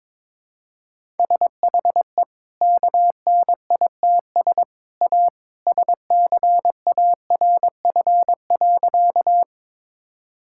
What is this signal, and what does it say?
Morse code, heard as a keyed tone, about 22 wpm, 700 Hz. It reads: SHE KNITS A SCARF.
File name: practice / 22wpm / Easy Sentences with Hints - 22wpm